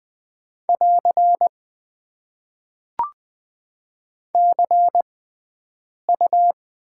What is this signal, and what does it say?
Morse code, heard as a keyed tone, about 20 wpm, spaced as for 7 wpm, 700 Hz. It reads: <AR> CU
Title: Top 90 QSO Elements - 6x Character Spacing - 20wpm